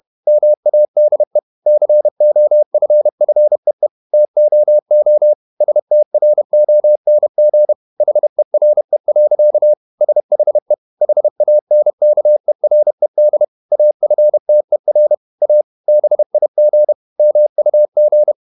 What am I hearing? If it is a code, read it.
Morse code, heard as a keyed tone, about 31 wpm, 600 Hz. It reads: MADE COFFEE TOO STRONG HERE. SHE HANKERED AFTER A BIG MUG
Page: Never 31wpm | CW med Gnister